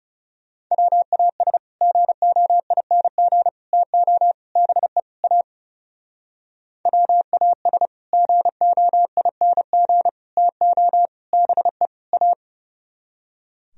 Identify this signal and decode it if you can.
Morse code, heard as a keyed tone, about 36 wpm, 700 Hz. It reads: WAS GOING TO BE A WAS GOING TO BE A
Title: Speed-Racing - Sets of 5 Words 30wpm